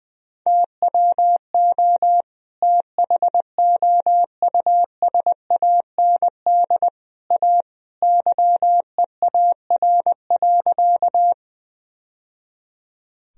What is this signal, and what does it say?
Morse code, heard as a keyed tone, about 20 wpm, 700 Hz. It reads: TWO THOUSAND A YEAR.
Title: Sentences from Top 1000 Words - Review Part 3 - 20wpm